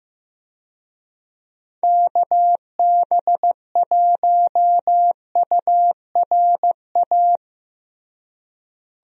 Morse code, heard as a keyed tone, about 16 wpm, 700 Hz. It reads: KB1URA
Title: USA Call Signs - 6 Characters - Repeated 3x - 15wpm